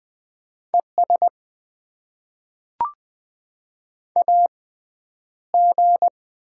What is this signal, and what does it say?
Morse code, heard as a keyed tone, about 20 wpm, spaced as for 10 wpm, 700 Hz. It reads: ES AG
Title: Top 20 QSO Elements - 6x Character Spacing - 20wpm